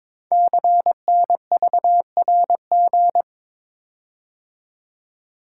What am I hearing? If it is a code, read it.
Morse code, heard as a keyed tone, about 22 wpm, 700 Hz. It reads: CNVRG
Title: Code Groups - 5 Letters - 22wpm